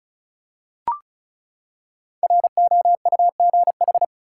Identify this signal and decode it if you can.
Morse code, heard as a keyed tone, about 34 wpm, 700 Hz. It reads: ROUGH
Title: Five Letter Words - 35wpm